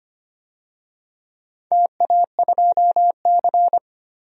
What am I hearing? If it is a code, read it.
Morse code, heard as a keyed tone, about 25 wpm, 700 Hz. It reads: TA2C
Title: DXCC-100 Call Signs - 4 Characters - 25wpm